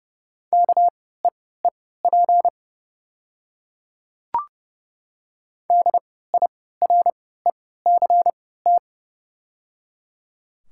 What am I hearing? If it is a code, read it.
Morse code, heard as a keyed tone, about 30 wpm, spaced as for 17 wpm, 690 Hz. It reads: KEEP DIRECT
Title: 3x Character Spacing - Top 500 Words - 30wpm